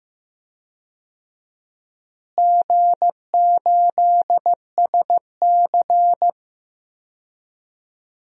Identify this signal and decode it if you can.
Morse code, heard as a keyed tone, about 15 wpm, 700 Hz. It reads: G8SC
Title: DXCC-100 Call Signs - 4 Characters - Repeated 3x - 15wpm